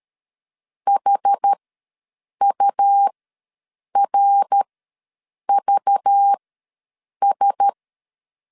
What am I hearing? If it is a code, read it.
Morse code, heard as a keyed tone, about 13 wpm, spaced as for 7 wpm, 800 Hz. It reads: HURVS
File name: LEZIONI / 6 PKX0 / 6 PKX ws 12wpm 3106